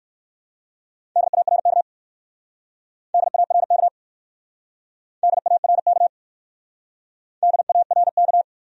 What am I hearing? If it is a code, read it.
Morse code, heard as a keyed tone, about 54 wpm, 700 Hz. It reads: DARK DARK DARK DARK
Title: Speed-Racing Top 500 Words 45wpm